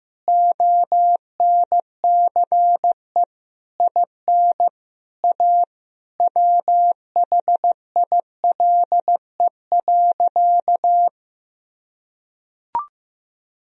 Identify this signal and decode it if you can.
Morse code, heard as a keyed tone, about 15 wpm, 700 Hz. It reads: ONCE IN A WHILE.
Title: Sentences from Top 500 Words and Abbreviations 15wpm